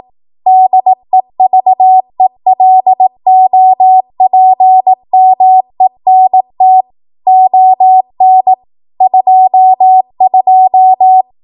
Morse code, heard as a keyed tone, about 18 wpm, 750 Hz. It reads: DEVELOPMENT ON 22
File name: Archive / 18 WPM / 18WPM